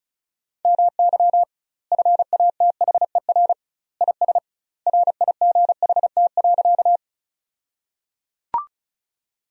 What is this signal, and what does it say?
Morse code, heard as a keyed tone, about 34 wpm, 700 Hz. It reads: MY FATHER IS RIGHT.
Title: 2x Word-Spacing - Sentences from Top 300 Words - 35wpm